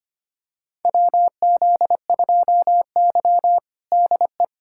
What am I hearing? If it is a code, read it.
Morse code, heard as a keyed tone, about 25 wpm, 700 Hz. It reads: WZ2Y DE
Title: USA Callsign DE Callsign - 4 Characters - 25wpm